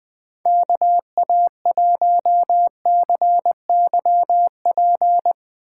Morse code, heard as a keyed tone, about 20 wpm, 700 Hz. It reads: KA1CYP
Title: USA Call Signs - 6 Characters - 8x Character Spacing - 20wpm